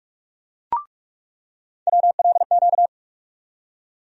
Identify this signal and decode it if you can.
Morse code, heard as a keyed tone, about 46 wpm, 700 Hz. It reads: WLQ
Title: USA Call Sign Postfixes - 3 Letters 45wpm